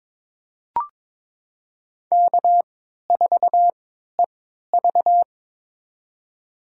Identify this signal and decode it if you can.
Morse code, heard as a keyed tone, about 22 wpm, spaced as for 13 wpm, 700 Hz. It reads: K4EV
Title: USA Call Signs - 4 Characters - 3x Character Spacing - 22wpm